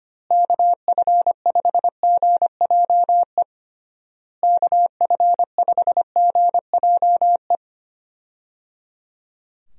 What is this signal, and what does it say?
Morse code, heard as a keyed tone, about 25 wpm, 700 Hz. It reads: KF5GJE KF5GJE
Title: USA Call Signs - 6 Characters - Repeated 3x - 25wpm